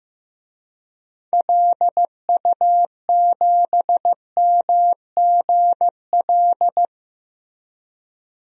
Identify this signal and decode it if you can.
Morse code, heard as a keyed tone, about 16 wpm, 700 Hz. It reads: LU7MGL
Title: DXCC-100 Call Signs - 6 Characters - Repeated 2x - 15wpm